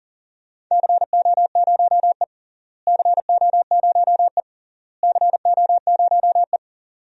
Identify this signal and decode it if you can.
Morse code, heard as a keyed tone, about 38 wpm, 700 Hz. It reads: CO0E CO0E CO0E
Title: DXCC-100 Call Signs - 4 Characters - Repeated 3x - 40wpm